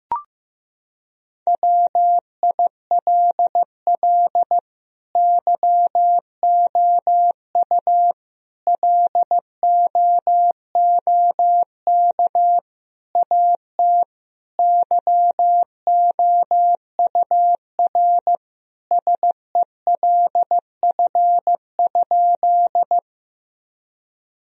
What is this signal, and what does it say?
Morse code, heard as a keyed tone, about 15 wpm, 700 Hz. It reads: WILL YOU LOOK AT YOUR SELF?
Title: Sentences from Top 1000 Words - Review Part 1 - 15wpm